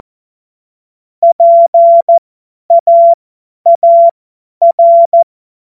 Morse code, heard as a keyed tone, about 14 wpm, spaced as for 10 wpm, 680 Hz. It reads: PAAR
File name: Morsekurs / Witze / Witz 25. Januar 2022, 50 F70